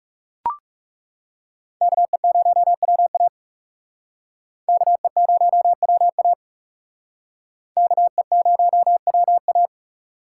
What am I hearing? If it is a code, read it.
Morse code, heard as a keyed tone, about 37 wpm, 700 Hz. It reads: KE0WA KE0WA KE0WA
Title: Speed Racing - USA 2x2 Call Signs 30wpm